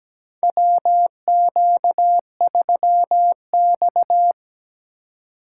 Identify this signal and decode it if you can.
Morse code, heard as a keyed tone, about 17 wpm, 700 Hz. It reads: WQ3X
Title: USA Call Signs - 4 Characters - 2x Character Spacing - 17wpm